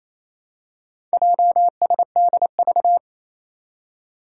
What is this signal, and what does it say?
Morse code, heard as a keyed tone, about 27 wpm, 700 Hz. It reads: JSDV
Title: Four Challenging Letter-Number Combinations 28wpm